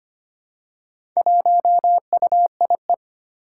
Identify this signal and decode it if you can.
Morse code, heard as a keyed tone, about 25 wpm, 700 Hz. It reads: 1UIE
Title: Four Challenging Letter-Number Combinations 25wpm